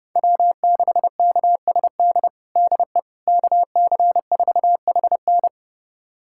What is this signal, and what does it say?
Morse code, heard as a keyed tone, about 30 wpm, 700 Hz. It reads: W6KSD DE KC4HN
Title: USA Callsign DE Callsign - 5 Characters - 30wpm